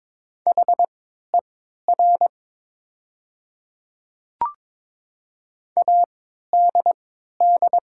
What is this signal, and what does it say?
Morse code, heard as a keyed tone, about 22 wpm, spaced as for 12 wpm, 700 Hz. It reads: HER ADD
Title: 3x Character Spacing - Top 200 Words - 22wpm